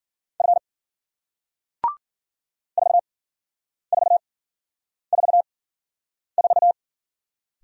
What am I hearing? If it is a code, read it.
Morse code, heard as a keyed tone, about 53 wpm, spaced as for 9 wpm, 700 Hz. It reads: F 4444